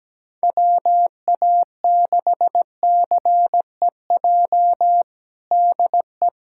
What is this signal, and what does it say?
Morse code, heard as a keyed tone, about 17 wpm, 700 Hz. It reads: WA6CEJ DE
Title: USA Callsign DE Callsign - 6 Characters - 17wpm